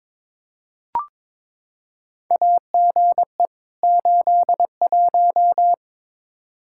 Morse code, heard as a keyed tone, about 21 wpm, 700 Hz. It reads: AGE 81
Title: Practice QSO - Exchange 3 - More Info - 22wpm